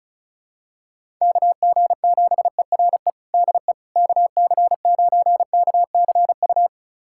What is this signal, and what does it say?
Morse code, heard as a keyed tone, about 36 wpm, 700 Hz. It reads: KG7ERE DE KC9KCU
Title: USA Callsign DE Callsign 35wpm